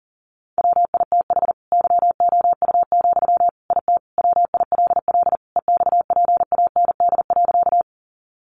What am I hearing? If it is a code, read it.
Morse code, heard as a keyed tone, about 40 wpm, 700 Hz. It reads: WITH YOU, IT WILL EXPAND.
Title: Sentences from Top 1400 Words 40wpm